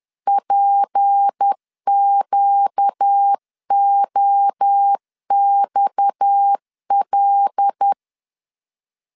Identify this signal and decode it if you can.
Morse code, heard as a keyed tone, about 10 wpm, 800 Hz. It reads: PQOXL